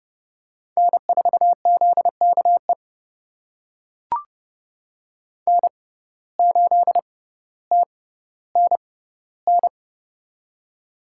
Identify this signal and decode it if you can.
Morse code, heard as a keyed tone, about 30 wpm, spaced as for 15 wpm, 700 Hz. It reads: N4ZKE N8TNN
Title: USA Call Signs - 5 Characters - 6x Character Spacing - 30wpm